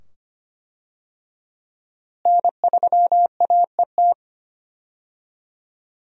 Morse code, heard as a keyed tone, about 25 wpm, 700 Hz. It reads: N3AET
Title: USA Call Signs - 5 Characters - Repeated 2x - 25wpm